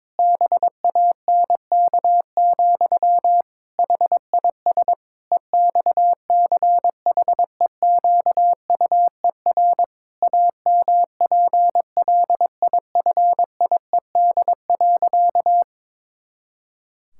Morse code, heard as a keyed tone, about 22 wpm, 700 Hz. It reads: BANK, HIS EXCHEQUER AMPLIFIED.
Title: Pangram Challenge v2 - 22wpm